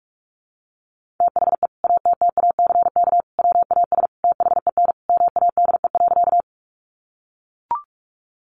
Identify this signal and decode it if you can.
Morse code, heard as a keyed tone, about 44 wpm, 700 Hz. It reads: THE ATTACK WAS THEN MADE.
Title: Sentences from Top 1500 Words 45wpm